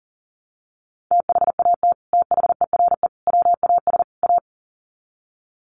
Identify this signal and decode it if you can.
Morse code, heard as a keyed tone, about 38 wpm, 700 Hz. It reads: THAT THERE WAS A
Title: Sets of 4 Words 40wpm